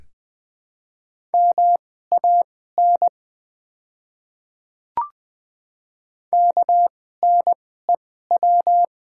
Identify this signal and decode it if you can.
Morse code, heard as a keyed tone, about 20 wpm, spaced as for 14 wpm, 700 Hz. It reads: MAN KNEW
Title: 2x Character Spacing - Top 500 Words - 20wpm